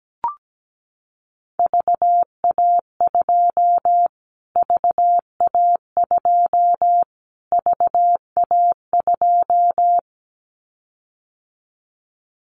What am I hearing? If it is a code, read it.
Morse code, heard as a keyed tone, about 17 wpm, 700 Hz. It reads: VA2 VA2 VA2
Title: Callsign Prefixes for Top 25 DXCC - Repeated 3x - 17wpm